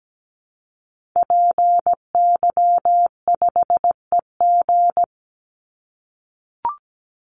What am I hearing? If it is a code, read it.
Morse code, heard as a keyed tone, about 17 wpm, 700 Hz. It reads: PY5EG
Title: Call Signs - 17wpm